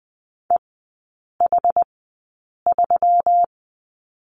Morse code, heard as a keyed tone, about 20 wpm, spaced as for 8 wpm, 700 Hz. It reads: EH3
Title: 121 Lesson 20e - Number 3 - Three Characters 20wpm